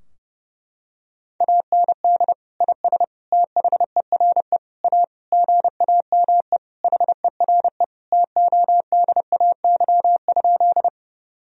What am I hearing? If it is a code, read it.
Morse code, heard as a keyed tone, about 30 wpm, 700 Hz. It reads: AND IS THERE A GAME HERE TODAY?
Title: Sentences from Top 1000 Words - Review Part 3 - 30wpm